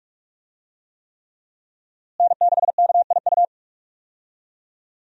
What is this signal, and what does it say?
Morse code, heard as a keyed tone, about 43 wpm, 700 Hz. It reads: N6KIU
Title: USA Call Signs - 5 Characters - Repeated 3x - 45wpm